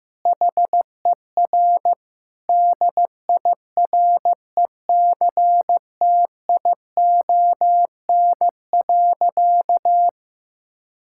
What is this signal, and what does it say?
Morse code, heard as a keyed tone, about 15 wpm, 700 Hz. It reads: HER DIRECTION.
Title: Sentences from Top 1000 Words - Review Part 5 - 15wpm